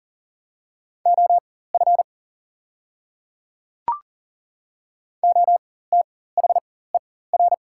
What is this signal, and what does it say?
Morse code, heard as a keyed tone, about 40 wpm, spaced as for 19 wpm, 700 Hz. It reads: OF OTHER
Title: 4x Character Spacing - Top 100 Words - 40wpm